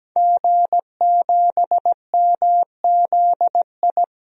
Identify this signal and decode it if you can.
Morse code, heard as a keyed tone, about 17 wpm, 700 Hz. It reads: G7MZI
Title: DXCC-100 Call Signs - 5 Characters - 17wpm